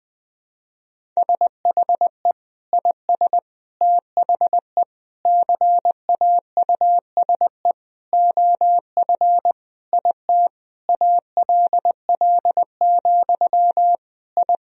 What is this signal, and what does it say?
Morse code, heard as a keyed tone, about 21 wpm, 700 Hz. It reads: SHE IS THE CAUSE OF IT ALL, I
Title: Sentences from Top 1000 Words - Review Part 1 - 20wpm